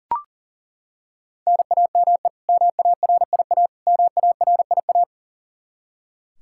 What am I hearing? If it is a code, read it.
Morse code, heard as a keyed tone, about 40 wpm, 700 Hz. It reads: NAME MARIA MARIA
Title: Practice QSO - Exchange 1 - Beginning - 40wpm